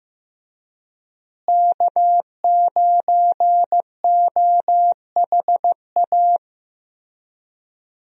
Morse code, heard as a keyed tone, about 15 wpm, 700 Hz. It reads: K9OHA
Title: USA Call Signs - 5 Characters - Repeated 3x - 15wpm